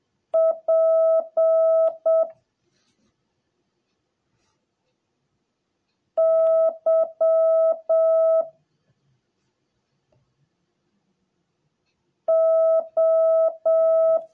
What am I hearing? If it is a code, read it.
Morse code, heard as a keyed tone, about 7 wpm, 650 Hz. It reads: P Y O